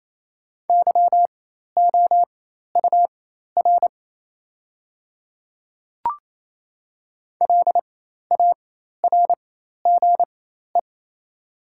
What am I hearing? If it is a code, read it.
Morse code, heard as a keyed tone, about 28 wpm, spaced as for 13 wpm, 700 Hz. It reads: YOUR LARGE
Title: 4x Character Spacing - Top 300 Words - 28wpm